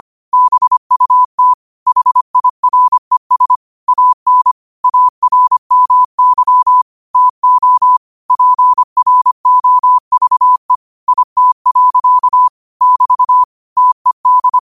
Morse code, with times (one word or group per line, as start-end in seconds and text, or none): 0.3-1.5 BUT
1.9-3.6 HIRES
3.9-4.5 AN
4.8-6.8 ARMY
7.2-8.0 TO
8.3-10.8 PROVE
11.1-12.5 IT.
12.8-13.4 <BT>
13.8-14.6 TED